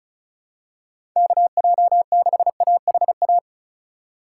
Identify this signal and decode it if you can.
Morse code, heard as a keyed tone, about 35 wpm, 700 Hz. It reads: KJ6AHA
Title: USA Call Signs - 6 Characters - 3x Character Spacing - 35wpm